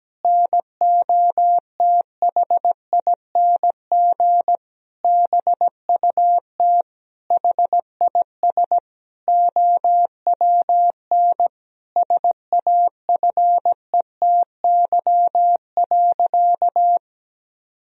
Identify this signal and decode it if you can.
Morse code, heard as a keyed tone, about 17 wpm, 700 Hz. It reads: NOTHING BUT HIS OWN SAFETY.
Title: Sentences from Top 1000 Words - Review Part 4 - 17wpm